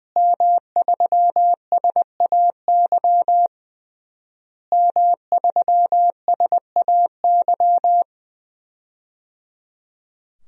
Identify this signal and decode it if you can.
Morse code, heard as a keyed tone, about 21 wpm, 700 Hz. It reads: M3SAY M3SAY
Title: DXCC-100 Call Signs - 5 Characters - Repeated 3x - 20wpm